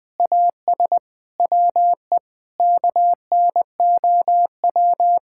Morse code, heard as a keyed tone, about 19 wpm, 700 Hz. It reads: AS WE KNOW
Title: Sentences with 8-Word Phrases - Spoken Beginning - 20wpm